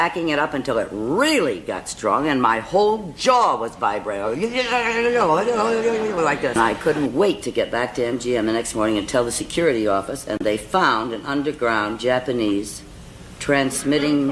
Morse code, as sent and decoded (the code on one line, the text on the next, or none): none